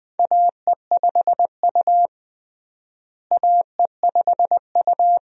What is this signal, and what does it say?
Morse code, heard as a keyed tone, about 20 wpm, 700 Hz. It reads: AE5U AE5U
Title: USA Call Signs - 4 Characters - Repeated 2x - 20wpm